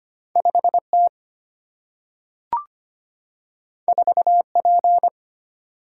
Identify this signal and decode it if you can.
Morse code, heard as a keyed tone, about 25 wpm, 700 Hz. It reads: 5T 4P
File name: Two Letter-Number Combinations - 25wpm